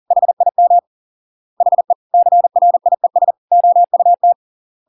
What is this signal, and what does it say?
Morse code, heard as a keyed tone, about 40 wpm, 700 Hz. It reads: HIM HE CRIES OUT